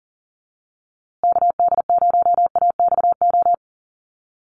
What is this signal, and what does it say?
Morse code, heard as a keyed tone, about 40 wpm, 700 Hz. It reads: KD0AXO
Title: USA Call Signs Repeated 2x 40wpm